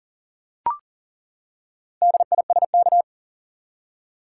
Morse code, heard as a keyed tone, about 40 wpm, 700 Hz. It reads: DISK
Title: Four Letter Words - 40wpm